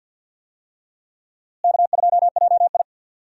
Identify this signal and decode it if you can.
Morse code, heard as a keyed tone, about 45 wpm, 700 Hz. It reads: K2JI
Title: USA Call Signs - 4 Characters - Repeated 3x - 50wpm